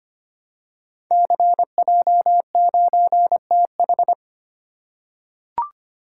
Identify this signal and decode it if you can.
Morse code, heard as a keyed tone, about 25 wpm, 700 Hz. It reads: CJ9TH
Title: Code Groups - 5 Letter:Numbers - 25wpm